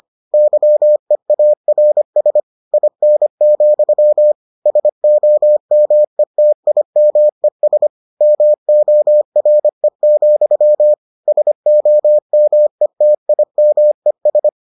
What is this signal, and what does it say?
Morse code, heard as a keyed tone, about 25 wpm, 600 Hz. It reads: YEARS IN, SOMETIMES MORE, SOMETIMES